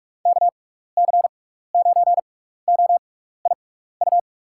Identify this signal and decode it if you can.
Morse code, heard as a keyed tone, about 47 wpm, spaced as for 16 wpm, 700 Hz. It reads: KC9OIU
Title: USA Call Signs - 6x Character Spacing - 45wpm